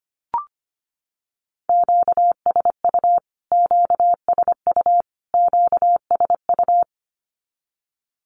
Morse code, heard as a keyed tone, about 25 wpm, 700 Hz. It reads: QSU QSU QSU
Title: Q-Codes Repeated 3x 25wpm